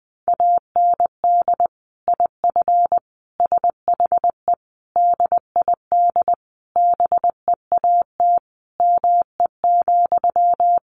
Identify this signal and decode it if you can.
Morse code, heard as a keyed tone, about 19 wpm, 700 Hz. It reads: AND IF SHE DID BEAT ME,
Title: Sentences from Top 1300 Words 20wpm